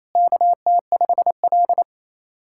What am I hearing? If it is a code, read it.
Morse code, heard as a keyed tone, about 28 wpm, 700 Hz. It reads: KT5L
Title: SOTA QSO Practice 28wpm